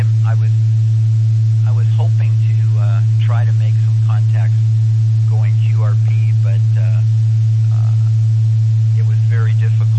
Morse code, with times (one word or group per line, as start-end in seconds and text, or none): none